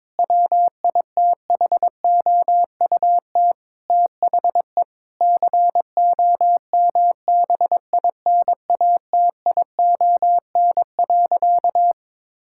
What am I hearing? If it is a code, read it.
Morse code, heard as a keyed tone, about 22 wpm, 700 Hz. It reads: WITHOUT THE COMBINATION.
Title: Sentences from Top 1000 Words - Review Part 6 - 22wpm